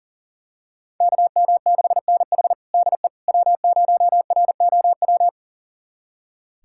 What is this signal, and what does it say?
Morse code, heard as a keyed tone, about 40 wpm, 700 Hz. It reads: KM6NH DE W0ROW
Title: USA Callsign DE Callsign - 5 Characters - 40wpm